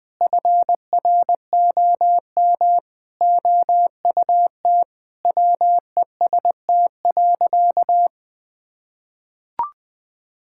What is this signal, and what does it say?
Morse code, heard as a keyed tone, about 20 wpm, 700 Hz. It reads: FROM OUT WEST.
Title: Sentences from Top 1000 Words - Review Part 5 - 20wpm